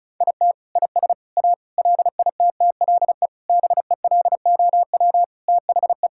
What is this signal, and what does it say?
Morse code, heard as a keyed tone, about 36 wpm, 700 Hz. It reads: IT IS A LITTLE BELOW THE
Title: Sentences from Top 1000 Words - Review Part 5 - 35wpm